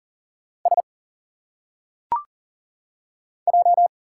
Morse code, heard as a keyed tone, about 40 wpm, 700 Hz. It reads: S J
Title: Single Letters - Phonetic Pronunciation 40wpm